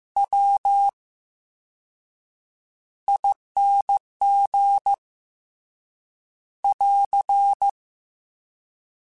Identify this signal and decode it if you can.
Morse code, heard as a keyed tone, about 15 wpm, 800 Hz. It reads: W I N G <AR>
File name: Special INGs-mp3s / Special INGs-mp3s / VerySlowToFaster Ing @13